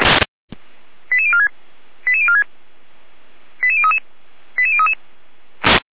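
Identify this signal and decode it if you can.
Five-tone selective call (ZVEI), heard as quick runs of five tones, tones of about 70 ms. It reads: 88046 88046 88033 88033
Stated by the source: ZVEI five-tone selective call 88139, received by SDR.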